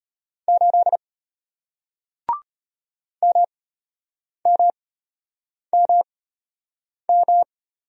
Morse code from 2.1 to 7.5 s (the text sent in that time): MMMM